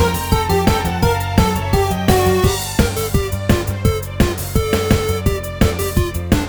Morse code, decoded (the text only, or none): none